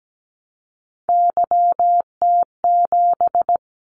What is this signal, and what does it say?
Morse code, heard as a keyed tone, about 17 wpm, 700 Hz. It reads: YT7
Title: Callsign Prefixes for Top 25 DXCC - 17wpm